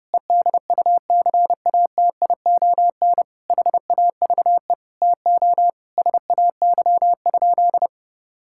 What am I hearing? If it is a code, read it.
Morse code, heard as a keyed tone, about 30 wpm, 700 Hz. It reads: EDUCATION HAVE TO SAY?
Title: Sentences from Top 1000 Words - Review Part 6 - 30wpm